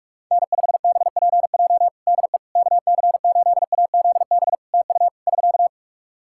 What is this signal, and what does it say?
Morse code, heard as a keyed tone, about 43 wpm, 700 Hz. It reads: N5BPJ DE KC8AZB TU <SK>
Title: K1USN SST QSO Practice 45wpm